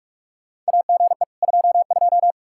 Morse code, heard as a keyed tone, about 46 wpm, 700 Hz. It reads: AGE 22
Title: Practice QSO - Exchange 3 - More Info - 45wpm